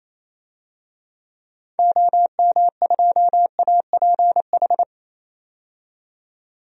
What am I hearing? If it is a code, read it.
Morse code, heard as a keyed tone, about 28 wpm, 700 Hz. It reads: OM2APH
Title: DXCC-100 Call Signs - 6 Characters - 28wpm